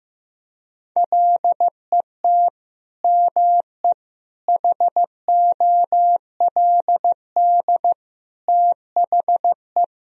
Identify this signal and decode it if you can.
Morse code, heard as a keyed tone, about 15 wpm, 700 Hz. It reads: LET ME HOLD THE
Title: Sentences from Top 1000 Words - Review Part 2 - 15wpm